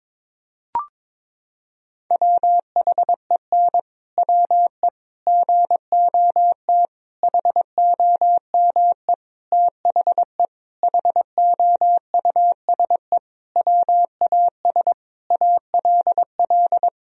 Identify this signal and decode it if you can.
Morse code, heard as a keyed tone, about 22 wpm, 700 Hz. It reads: WHEN WE GOT HOME THE HOUSE WAS ALL
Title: Sentences from Top 500 Words and Abbreviations 22wpm